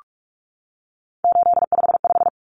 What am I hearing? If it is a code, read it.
Morse code, heard as a keyed tone, about 44 wpm, 700 Hz. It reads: 855
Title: Numbers - 3-Digit 45wpm